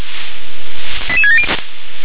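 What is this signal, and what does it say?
Five-tone selective call (ZVEI), heard as quick runs of five tones, tones of about 70 ms. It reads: 88670
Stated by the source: ZVEI five-tone selective call received by SDR.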